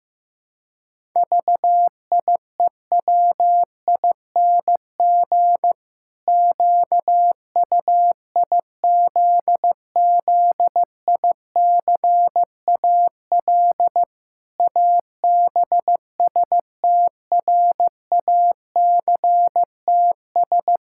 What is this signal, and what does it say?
Morse code, heard as a keyed tone, about 16 wpm, 700 Hz. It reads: VIEWING QUIZZICAL ABSTRACTS
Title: Pangram Challenge v2 - 15wpm